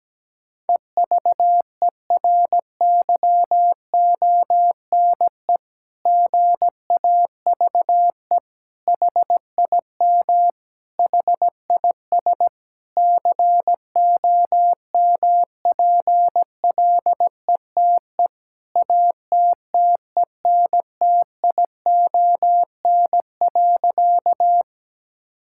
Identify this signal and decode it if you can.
Morse code, heard as a keyed tone, about 17 wpm, 700 Hz. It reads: EVERYONE GAVE HIM HIS COMPLETE ATTENTION.
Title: Sentences from Top 1000 Words - Review Part 6 - 17wpm